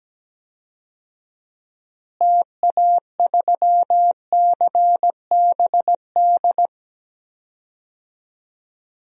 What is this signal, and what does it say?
Morse code, heard as a keyed tone, about 17 wpm, 700 Hz. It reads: TA3CBD
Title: DXCC-100 Call Signs - 6 Characters - Repeated 3x - 17wpm